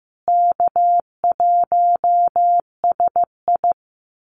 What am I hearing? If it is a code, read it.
Morse code, heard as a keyed tone, about 15 wpm, 700 Hz. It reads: K1SI
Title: USA Call Signs Repeated 3x 15wpm